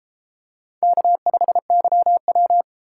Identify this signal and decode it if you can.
Morse code, heard as a keyed tone, about 33 wpm, 700 Hz. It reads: K5YW